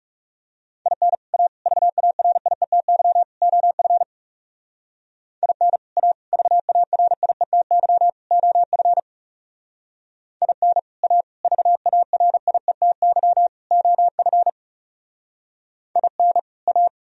Morse code, heard as a keyed tone, about 37 wpm, 700 Hz. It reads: IN A VARIETY OF IN A VARIETY OF IN A VARIETY OF IN A
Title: Speed-Racing - Sets of 4 Words 30wpm